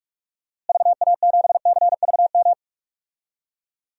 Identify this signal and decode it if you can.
Morse code, heard as a keyed tone, about 46 wpm, 700 Hz. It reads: VA7CVM
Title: DXCC-100 Call Signs - 6 Characters - Repeated 2x - 45wpm